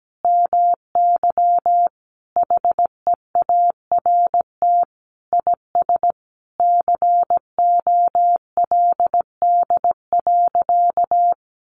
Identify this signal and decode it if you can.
Morse code, heard as a keyed tone, about 17 wpm, 700 Hz. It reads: MY HEART IS COLD.
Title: Sentences from Top 1200 Words 17wpm